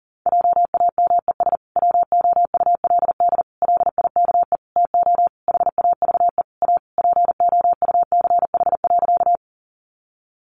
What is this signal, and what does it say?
Morse code, heard as a keyed tone, about 40 wpm, 700 Hz. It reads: JAMES WOULD LIKE TO HAVE A POUCH.
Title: Easy Phrases 40wpm